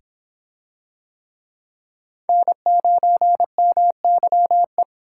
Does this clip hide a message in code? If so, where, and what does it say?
Yes: Morse code from 2.3 to 4.8 s, heard as a keyed tone, about 25 wpm, 700 Hz. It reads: N9MYE